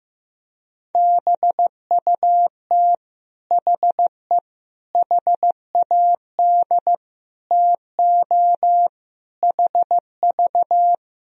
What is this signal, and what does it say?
Morse code, heard as a keyed tone, about 15 wpm, 700 Hz. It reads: BUT HE HAD TO HV
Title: Sentences from Top 100 Words and Abbreviations 15wpm